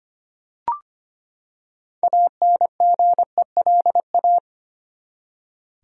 Morse code, heard as a keyed tone, about 25 wpm, 700 Hz. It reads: ANGELA
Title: USA Common Names - 25wpm